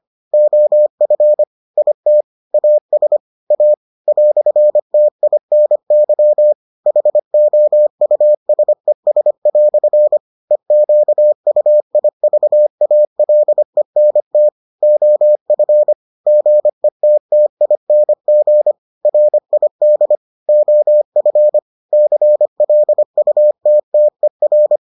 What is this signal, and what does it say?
Morse code, heard as a keyed tone, about 25 wpm, 600 Hz. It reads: OF IT AS A "TINY HOUSES" EQUIVALENT OF GETTING RID OF CLUTTER